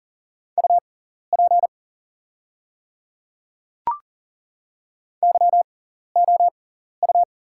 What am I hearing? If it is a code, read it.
Morse code, heard as a keyed tone, about 42 wpm, spaced as for 14 wpm, 690 Hz. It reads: UP YOU
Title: 6x Character Spacing - Top 100 Words - 40wpm